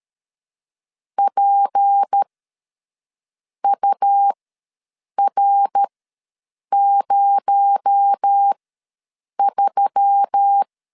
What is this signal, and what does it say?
Morse code, heard as a keyed tone, about 13 wpm, spaced as for 6 wpm, 800 Hz. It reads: PUR03
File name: LEZIONI / 8 YJ45 / 8 Y ws 12wpm 3409